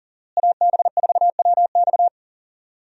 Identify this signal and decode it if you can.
Morse code, heard as a keyed tone, about 38 wpm, 700 Hz. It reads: AB4WX
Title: USA Call Signs - 5 Characters - Repeated 3x - 40wpm